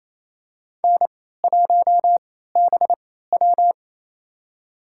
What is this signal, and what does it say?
Morse code, heard as a keyed tone, about 28 wpm, spaced as for 16 wpm, 700 Hz. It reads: N1BW
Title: USA Call Signs - 4 Characters - 3x Character Spacing - 28wpm